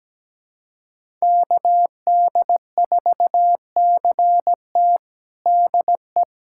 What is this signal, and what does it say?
Morse code, heard as a keyed tone, about 17 wpm, 700 Hz. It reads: KD4CT DE
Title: USA Callsign DE Callsign - 5 Characters - 17wpm